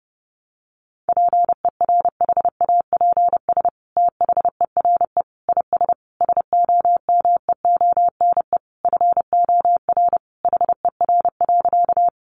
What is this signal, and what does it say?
Morse code, heard as a keyed tone, about 30 wpm, 700 Hz. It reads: PERHAPS THERE IS SOMEONE FOR HER.
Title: Sentences from Top 600 Words 30wpm